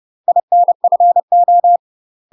Morse code, heard as a keyed tone, about 30 wpm, 700 Hz. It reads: INFO
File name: UK MASTER QSOs TRAINING / 30WPM / UK MASTER QSOs 0016